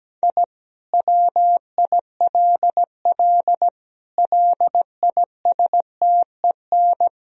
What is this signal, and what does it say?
Morse code, heard as a keyed tone, about 17 wpm, 700 Hz. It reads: I WILL LISTEN
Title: Sentences from Top 1000 Words - Review Part 3 - 17wpm